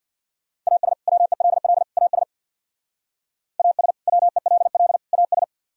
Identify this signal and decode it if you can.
Morse code, heard as a keyed tone, about 64 wpm, 700 Hz. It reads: AS WELL AS AS WELL AS